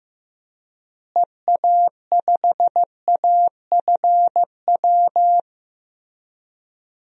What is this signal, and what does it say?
Morse code, heard as a keyed tone, about 16 wpm, 700 Hz. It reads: EA5AFW
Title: DXCC-100 Call Signs - 6 Characters - Repeated 2x - 15wpm